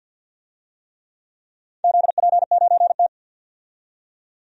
Morse code, heard as a keyed tone, about 53 wpm, 700 Hz. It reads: ZP9T
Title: DXCC-100 Call Signs - 4 Characters - 50wpm